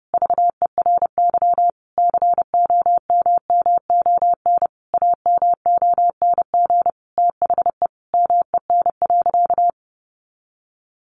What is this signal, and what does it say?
Morse code, heard as a keyed tone, about 30 wpm, 700 Hz. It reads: VERY COMMON AMONG THE MEN.
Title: Sentences from Top 1400 Words 30wpm